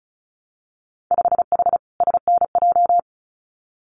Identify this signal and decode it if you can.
Morse code, heard as a keyed tone, about 36 wpm, 700 Hz. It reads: 5H SNJ
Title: Winter Field Day Exchanges Encore 35wpm